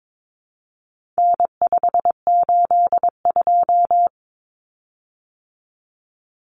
Morse code, heard as a keyed tone, about 22 wpm, 700 Hz. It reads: N582
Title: Common Cut Numbers - 4-Digit 22wpm